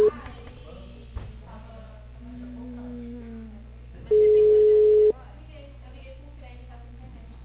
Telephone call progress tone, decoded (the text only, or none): ringback tone